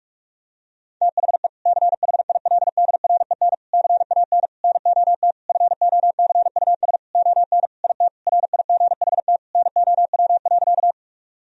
Morse code, heard as a keyed tone, about 46 wpm, 700 Hz. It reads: THE CHILDREN CAN NOT FOCUS ON IT RIGHT NOW.
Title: Sentences from Top 1000 Words - Review Part 4 - 45wpm